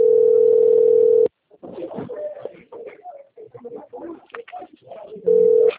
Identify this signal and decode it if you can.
Ringback tone (North America), 440 + 480 Hz.